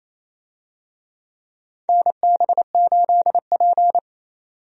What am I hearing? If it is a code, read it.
Morse code, heard as a keyed tone, about 28 wpm, 700 Hz. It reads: NB8P
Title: USA Call Signs - 4 Characters - 28wpm